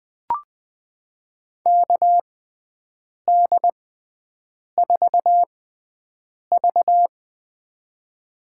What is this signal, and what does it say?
Morse code, heard as a keyed tone, about 20 wpm, spaced as for 7 wpm, 700 Hz. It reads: KD4V
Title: USA Call Signs - 4 Characters - 6x Character Spacing - 20wpm